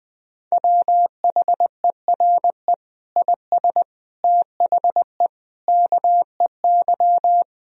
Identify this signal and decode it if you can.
Morse code, heard as a keyed tone, about 19 wpm, 700 Hz. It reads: WHERE IS THE KEY
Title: Sentences from Top 1000 Words - Review Part 4 - 20wpm